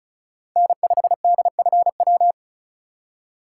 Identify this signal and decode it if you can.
Morse code, heard as a keyed tone, about 34 wpm, 700 Hz. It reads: N5DFW
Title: USA Call Signs - 5 Characters - 8x Character Spacing - 35wpm